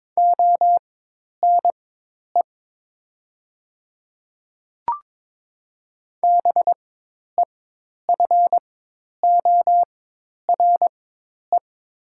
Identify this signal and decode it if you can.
Morse code, heard as a keyed tone, about 22 wpm, spaced as for 10 wpm, 700 Hz. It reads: ONE BEFORE
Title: 4x Character Spacing - Top 200 Words - 22wpm